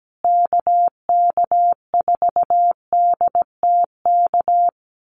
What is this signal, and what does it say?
Morse code, heard as a keyed tone, about 17 wpm, 700 Hz. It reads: KK4DTK
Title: USA Call Signs Repeated 2x 17wpm